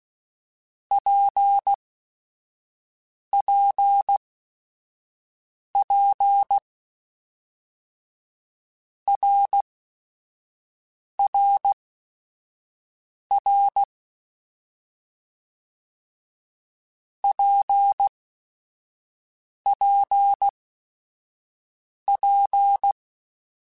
Morse code, heard as a keyed tone, about 16 wpm, spaced as for 5 wpm, 800 Hz. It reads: PPP RRR PPP
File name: Basic Sets / Basic Letter Groups Collective7 / RKPX@5